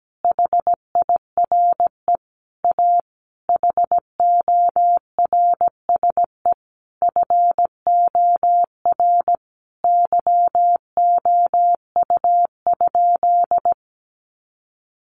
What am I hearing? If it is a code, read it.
Morse code, heard as a keyed tone, about 17 wpm, 700 Hz. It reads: HIRE A HORSE FOR YOU?
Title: Sentences from Top 1400 Words 17wpm